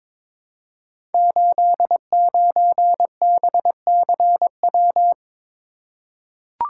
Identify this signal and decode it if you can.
Morse code, heard as a keyed tone, about 23 wpm, 700 Hz. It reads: 89BCW
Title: Code Groups - 5 Letter:Numbers - 22wpm